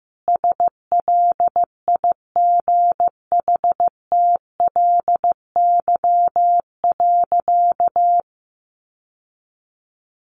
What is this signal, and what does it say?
Morse code, heard as a keyed tone, about 15 wpm, 700 Hz. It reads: SLIGHTLY.
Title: Sentences from Top 1300 Words 15wpm